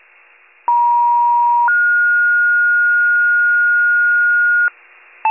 Two-tone page, heard as a one-second tone, then a three-second tone, about 950 then 1430 Hz, with noise. It is heard from 0.7 to 4.7 s.